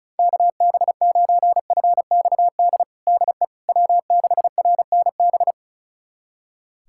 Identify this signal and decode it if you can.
Morse code, heard as a keyed tone, about 36 wpm, 700 Hz. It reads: KB9FXD DE W6RNB
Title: USA Callsign DE Callsign 35wpm